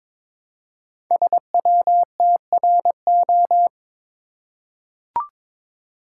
Morse code, heard as a keyed tone, about 23 wpm, 700 Hz. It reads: SWTRO
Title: Code Groups - 5 Letters - 22wpm